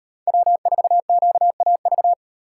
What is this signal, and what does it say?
Morse code, heard as a keyed tone, about 38 wpm, 700 Hz. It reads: W4QAV